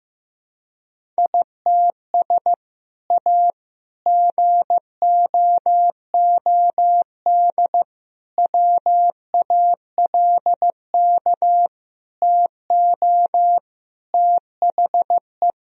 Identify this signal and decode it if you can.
Morse code, heard as a keyed tone, about 15 wpm, 700 Hz. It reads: ITS A GOOD WALK TO THE
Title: Sentences from Top 1000 Words - Review Part 1 - 15wpm